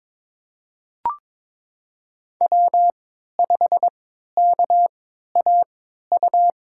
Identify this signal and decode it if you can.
Morse code, heard as a keyed tone, about 22 wpm, spaced as for 13 wpm, 700 Hz. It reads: W5KAU
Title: USA Call Signs - 5 Characters - 3x Character Spacing - 22wpm